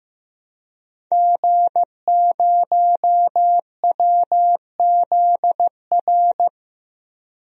Morse code, heard as a keyed tone, about 15 wpm, 700 Hz. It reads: G0WZR
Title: DXCC-100 Call Signs - 5 Characters - 15wpm